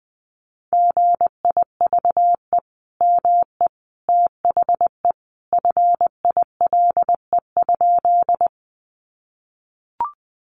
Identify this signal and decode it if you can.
Morse code, heard as a keyed tone, about 20 wpm, 700 Hz. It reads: GIVE ME THE FILE?
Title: Sentences from Top 1200 Words 20wpm